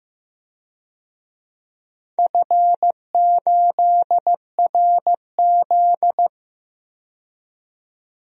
Morse code, heard as a keyed tone, about 15 wpm, 700 Hz. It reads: F8RZ
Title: DXCC-100 Call Signs - 4 Characters - 15wpm